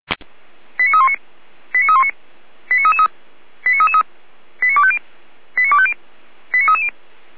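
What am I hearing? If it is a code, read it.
Five-tone selective call (ZVEI), heard as quick runs of five tones, tones of about 80 ms. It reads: 89319 89319 89333 89333 89250 89250 89330
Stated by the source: ZVEI five-tone call 89290 received on SDR.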